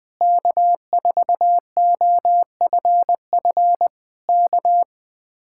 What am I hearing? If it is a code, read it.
Morse code, heard as a keyed tone, about 20 wpm, 700 Hz. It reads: K4OFF K